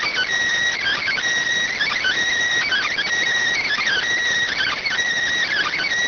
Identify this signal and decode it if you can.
Emergency Alert System (SAME) tones (header bursts), heard as harsh buzzing bursts. It is heard from 0.8 to 5.0 s.